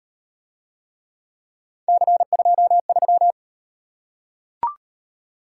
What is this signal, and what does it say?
Morse code, heard as a keyed tone, about 38 wpm, 700 Hz. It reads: C23